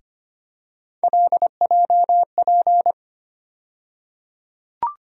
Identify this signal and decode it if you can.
Morse code, heard as a keyed tone, about 25 wpm, 700 Hz. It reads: LJP
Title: USA Call Sign Postfixes - 3 Letters 25wpm